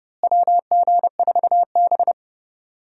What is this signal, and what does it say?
Morse code, heard as a keyed tone, about 30 wpm, 700 Hz. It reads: WG4B
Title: USA Call Signs - 4 Characters - Repeated 2x - 30wpm